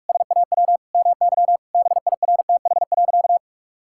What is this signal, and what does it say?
Morse code, heard as a keyed tone, about 41 wpm, 700 Hz. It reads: SAW MY BIRTH.
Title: Sentences from Top 1000 Words - Review Part 5 - 45wpm